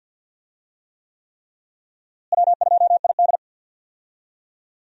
Morse code, heard as a keyed tone, about 48 wpm, 700 Hz. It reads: W2ID
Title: USA Call Signs - 4 Characters - Repeated 3x - 50wpm